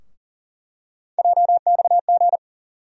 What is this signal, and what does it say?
Morse code, heard as a keyed tone, about 40 wpm, 700 Hz. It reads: JXG
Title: USA Call Sign Postfixes - 3 Letters 40wpm